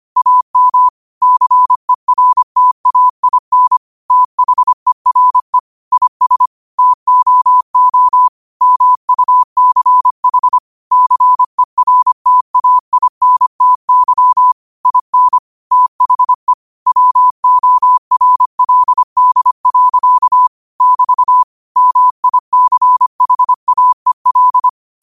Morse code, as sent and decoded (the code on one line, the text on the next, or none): .- -- / -.-. .-. - .- .. -. / - .... .-. / .. ... / - --- --- / -- ..- -.-. .... / -.-. .-. - .- .. -. - -.-- / .. -. / - .... / .-- --- .-. .-.. -.. .-.-.- / -...- / -- .. -.-. .... .- . .-..
AM CERTAIN THERE IS TOO MUCH CERTAINTY IN THE WORLD. <BT> MICHAEL